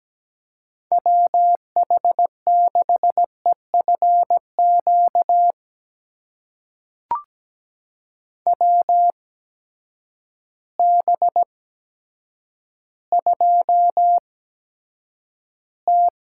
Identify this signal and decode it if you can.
Morse code, heard as a keyed tone, about 17 wpm, spaced as for 9 wpm, 700 Hz. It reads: WH6EFQ WB2T
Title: USA Call Signs - 8x Character Spacing - 17wpm